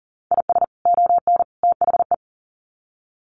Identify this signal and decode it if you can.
Morse code, heard as a keyed tone, about 40 wpm, 700 Hz. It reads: IS ON THE
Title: Sets of 3 Words 40wpm